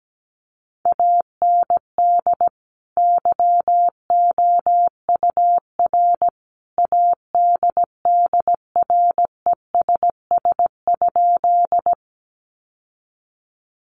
Morse code, heard as a keyed tone, about 17 wpm, 700 Hz. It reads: AND YOUR ADDRESS?
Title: Sentences from Top 1000 Words 17wpm